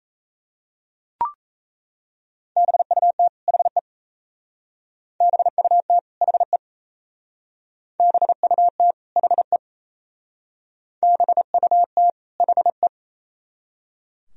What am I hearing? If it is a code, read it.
Morse code, heard as a keyed tone, about 36 wpm, 700 Hz. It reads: BUT HE BUT HE BUT HE BUT HE